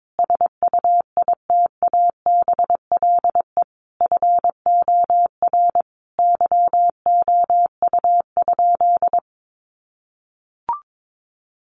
Morse code, heard as a keyed tone, about 22 wpm, 700 Hz. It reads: SUITABLE FOR YOU?
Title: Sentences from Top 1400 Words 22wpm